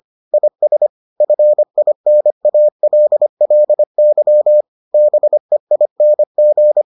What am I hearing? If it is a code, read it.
Morse code, heard as a keyed tone, about 25 wpm, 600 Hz. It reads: IS FINALLY BEING